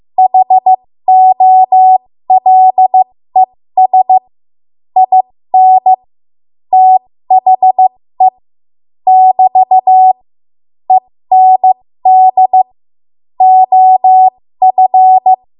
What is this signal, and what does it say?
Morse code, heard as a keyed tone, about 15 wpm, spaced as for 13 wpm, 750 Hz. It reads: HOLES IN THE <BT> END OF
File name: Morse / 13WPM